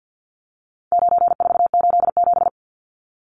Morse code, heard as a keyed tone, about 50 wpm, 700 Hz. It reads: 9487
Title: Numbers - 4-Digit 50wpm